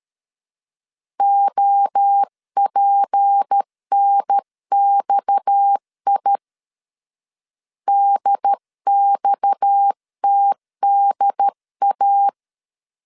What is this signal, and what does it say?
Morse code, heard as a keyed tone, about 13 wpm, 800 Hz. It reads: OPNXI DXTDA
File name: CW / DET / 1 6 1500C S C 12wpm 3533